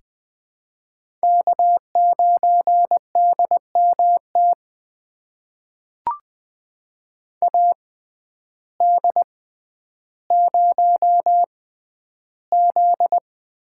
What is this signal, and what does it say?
Morse code, heard as a keyed tone, about 20 wpm, spaced as for 11 wpm, 700 Hz. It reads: K9DMT AD0Z
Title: USA Call Signs - 5 Characters - 6x Character Spacing - 20wpm